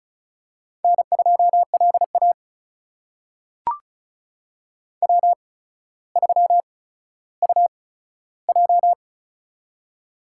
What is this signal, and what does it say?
Morse code, heard as a keyed tone, about 36 wpm, spaced as for 15 wpm, 700 Hz. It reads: N2LA W3UJ
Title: USA Call Signs - 4 Characters - 8x Character Spacing - 35wpm